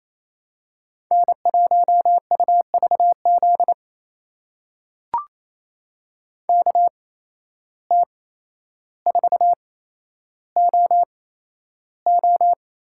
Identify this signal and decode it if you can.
Morse code, heard as a keyed tone, about 28 wpm, spaced as for 12 wpm, 700 Hz. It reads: N1UVZ KT4OO
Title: USA Call Signs - 5 Characters - 8x Character Spacing - 28wpm